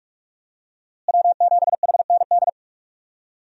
Morse code, heard as a keyed tone, about 43 wpm, 700 Hz. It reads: W7HND
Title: USA Call Signs - 5 Characters - Repeated 2x - 45wpm